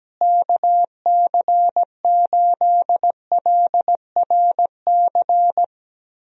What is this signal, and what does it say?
Morse code, heard as a keyed tone, about 17 wpm, 700 Hz. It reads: KC8LRC
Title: USA Call Signs - 6 Characters - Repeated 3x - 17wpm